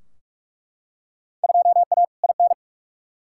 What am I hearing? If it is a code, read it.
Morse code, heard as a keyed tone, about 42 wpm, 700 Hz. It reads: 2A IN
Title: Field Day Exchanges - 45wpm